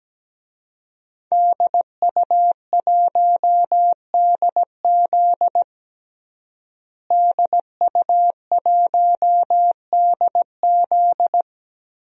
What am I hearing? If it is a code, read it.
Morse code, heard as a keyed tone, about 17 wpm, 700 Hz. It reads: DU1DZ DU1DZ
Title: DXCC-100 Call Signs - 5 Characters - Repeated 3x - 17wpm